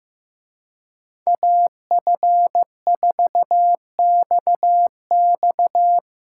Morse code, heard as a keyed tone, about 16 wpm, 700 Hz. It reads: AF4XX
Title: Speed Racing - USA 2x2 Call Signs 15wpm